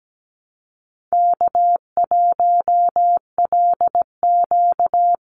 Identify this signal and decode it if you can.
Morse code, heard as a keyed tone, about 17 wpm, 700 Hz. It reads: K1LQ
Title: USA Call Signs 17wpm